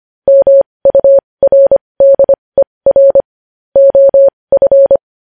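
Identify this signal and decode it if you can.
Morse code, heard as a keyed tone, about 25 wpm, spaced as for 20 wpm, 550 Hz. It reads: MURDER OF
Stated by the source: Morse code news headlines sent at 20 wpm.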